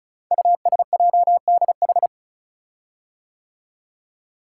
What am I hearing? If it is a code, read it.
Morse code, heard as a keyed tone, about 36 wpm, 690 Hz. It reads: USJDH
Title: Code Groups - 5 Letters - 35wpm